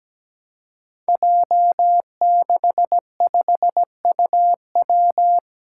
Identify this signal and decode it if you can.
Morse code, heard as a keyed tone, about 17 wpm, 700 Hz. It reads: J65UW
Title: Five Challenging Letter-Number Combinations 17wpm